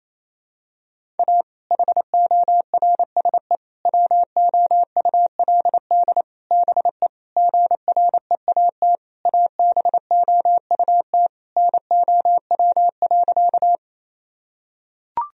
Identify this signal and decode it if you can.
Morse code, heard as a keyed tone, about 28 wpm, 700 Hz. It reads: A HORSE WOULD BE GREAT ABOUT NOW.
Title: Sentences from Top 1000 Words - Review Part 1 - 28wpm